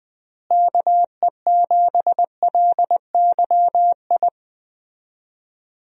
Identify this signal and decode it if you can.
Morse code, heard as a keyed tone, about 20 wpm, 700 Hz. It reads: KE7LYI
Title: USA Call Signs - 6 Characters - Repeated 2x - 20wpm